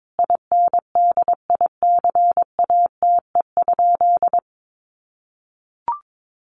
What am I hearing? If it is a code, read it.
Morse code, heard as a keyed tone, about 22 wpm, 700 Hz. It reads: INDICATE?
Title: Sentences from Top 1300 Words 22wpm